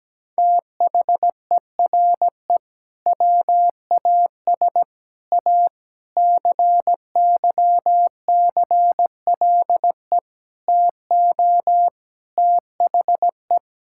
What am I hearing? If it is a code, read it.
Morse code, heard as a keyed tone, about 17 wpm, 700 Hz. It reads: THERE WAS A CYCLE TO THE
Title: Sentences from Top 1000 Words - Review Part 6 - 17wpm